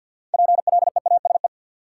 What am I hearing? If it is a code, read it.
Morse code, heard as a keyed tone, about 53 wpm, 700 Hz. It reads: PLEASE
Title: Sentences from Top 1500 Words - New Word Drill - 50wpm